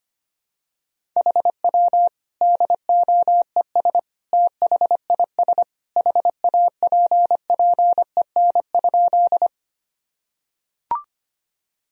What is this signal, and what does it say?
Morse code, heard as a keyed tone, about 25 wpm, 700 Hz. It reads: HW DOES THIS HAPPEN?
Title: Sentences from Top 400 Words and Abbreviations 25wpm